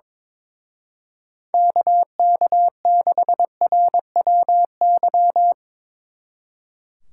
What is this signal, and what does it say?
Morse code, heard as a keyed tone, about 24 wpm, 700 Hz. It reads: KK6RWY
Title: Speed Racing - USA Call Signs 22wpm